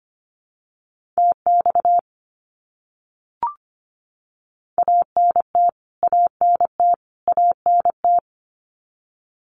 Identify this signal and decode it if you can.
Morse code, heard as a keyed tone, about 25 wpm, 700 Hz. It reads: TX ANT ANT ANT
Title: Abbreviations Repeated 3x - 25wpm